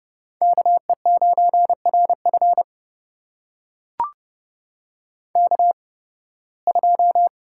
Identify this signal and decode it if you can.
Morse code, heard as a keyed tone, about 32 wpm, spaced as for 20 wpm, 700 Hz. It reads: KE9RF K2
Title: USA Call Signs - 5 Characters - 8x Character Spacing - 30wpm